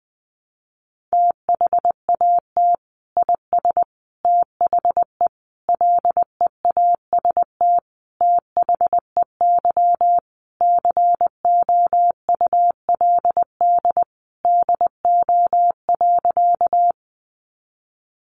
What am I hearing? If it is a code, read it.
Morse code, heard as a keyed tone, about 19 wpm, 700 Hz. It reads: THAT IS THE LEAST THEY COULD DO.
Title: Sentences from Top 700 Words 20wpm